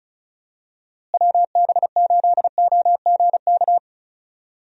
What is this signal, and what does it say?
Morse code, heard as a keyed tone, about 34 wpm, 700 Hz. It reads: WB8OGK
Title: USA Call Signs - 6 Characters - Repeated 3x - 35wpm